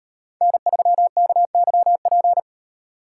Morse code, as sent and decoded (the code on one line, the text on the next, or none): -. ...-- -.- -.-- .--.
N3KYP